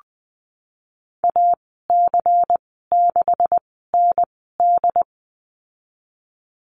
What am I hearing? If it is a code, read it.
Morse code, heard as a keyed tone, about 20 wpm, spaced as for 15 wpm, 700 Hz. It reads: AC6ND
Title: 179 Lesson 29g - Slash - Call Signs 20wpm